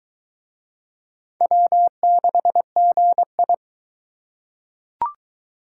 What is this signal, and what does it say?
Morse code, heard as a keyed tone, about 19 wpm, 700 Hz. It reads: W6GI